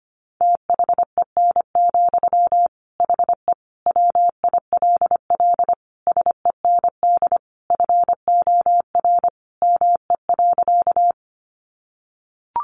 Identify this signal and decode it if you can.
Morse code, heard as a keyed tone, about 25 wpm, 700 Hz. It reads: THEN, HE WILL SEND FOR ME.
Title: Sentences from Top 1500 Words 25wpm